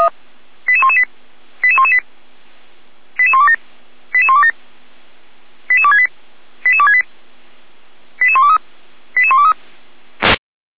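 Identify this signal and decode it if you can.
Five-tone selective call (ZVEI), heard as quick runs of five tones, tones of about 70 ms. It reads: 88118 88118 88127 88127 88268 88268 88123 88123
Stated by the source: Received on an SDR, ZVEI five-tone selective call, from the Feuerwehr Orlamünde (fire brigade) audio archive.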